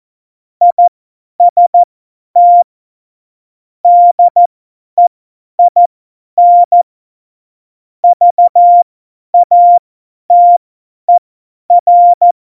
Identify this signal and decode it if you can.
Morse code, heard as a keyed tone, about 14 wpm, spaced as for 10 wpm, 700 Hz. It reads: IST DEIN VATER
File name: Morsekurs / Witze / Witz 03. November 2020, 50 F70